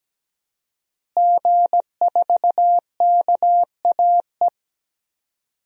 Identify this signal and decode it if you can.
Morse code, heard as a keyed tone, about 17 wpm, 700 Hz. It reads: G4KAE
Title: DXCC-100 Call Signs - 5 Characters - Repeated 2x - 17wpm